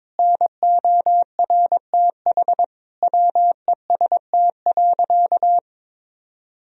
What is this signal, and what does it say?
Morse code, heard as a keyed tone, about 22 wpm, 700 Hz. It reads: NORTH WEST.
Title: Sentences from Top 1000 Words - Review Part 2 - 22wpm